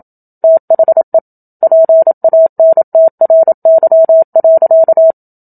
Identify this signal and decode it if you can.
Morse code, heard as a keyed tone, about 27 wpm, 650 Hz. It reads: THE PANTRY.
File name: Book Two / 01-Chapter 1 - 27 WPM